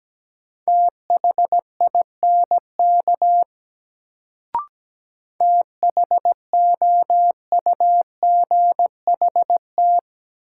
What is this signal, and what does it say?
Morse code, heard as a keyed tone, about 17 wpm, 700 Hz. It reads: THINK THOUGHT
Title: Sets of 2 Words - New Word Drill - 17wpm